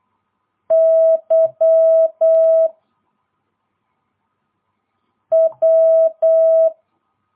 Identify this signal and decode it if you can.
Morse code, heard as a keyed tone, about 8 wpm, 650 Hz. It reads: Y W
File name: jh8cbh / 2022cw